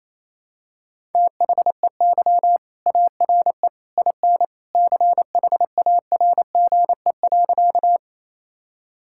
Morse code, heard as a keyed tone, about 28 wpm, 700 Hz. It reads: THEY ARE IN CHARGE.
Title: Sentences from Top 1000 Words - Review Part 6 - 28wpm